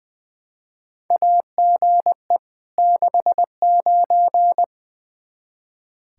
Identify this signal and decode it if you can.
Morse code, heard as a keyed tone, about 20 wpm, 700 Hz. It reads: AGE 69
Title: Practice QSO - Exchange 3 - More Info - 20wpm